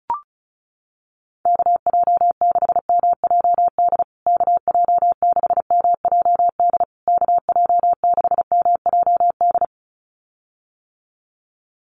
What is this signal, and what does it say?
Morse code, heard as a keyed tone, about 34 wpm, 700 Hz. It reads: KJ6MJD KJ6MJD KJ6MJD
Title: USA Call Signs Repeated 3x 35wpm